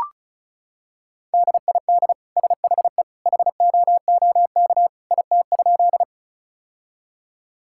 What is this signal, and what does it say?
Morse code, heard as a keyed tone, about 36 wpm, 700 Hz. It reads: DID SHE HOOK IT?
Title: Sentences from Top 1600 Words 35wpm